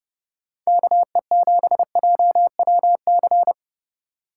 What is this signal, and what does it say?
Morse code, heard as a keyed tone, about 29 wpm, 700 Hz. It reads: KE7JWC